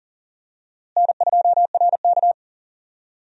Morse code, heard as a keyed tone, about 37 wpm, 700 Hz. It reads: N2RK
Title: USA Call Signs - 4 Characters - Repeated 2x - 40wpm